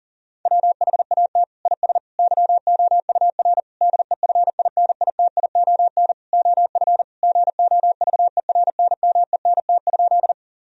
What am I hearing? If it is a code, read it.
Morse code, heard as a keyed tone, about 40 wpm, 700 Hz. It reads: WHAT IS YOUR DEFINITION OF GOVERNMENT?
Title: Sentences from Top 1000 Words - Review Part 4 - 40wpm